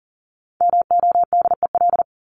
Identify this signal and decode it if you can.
Morse code, heard as a keyed tone, about 40 wpm, 700 Hz. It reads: MODEL
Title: Top 501-750 Words 40wpm